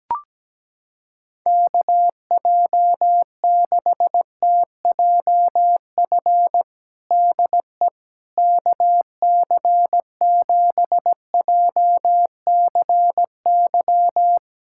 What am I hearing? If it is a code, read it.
Morse code, heard as a keyed tone, about 17 wpm, 700 Hz. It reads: KJ6TJF DE KC7JCY
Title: USA Callsign DE Callsign - 6 Characters - 17wpm